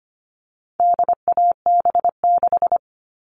Morse code, heard as a keyed tone, about 25 wpm, 700 Hz. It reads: DAB6
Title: All Cut Numbers - 4-Digit 25wpm